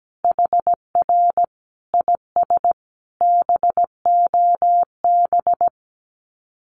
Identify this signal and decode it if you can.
Morse code, heard as a keyed tone, about 17 wpm, 700 Hz. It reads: HR IS BOB
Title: Common QSO Phrases 17wpm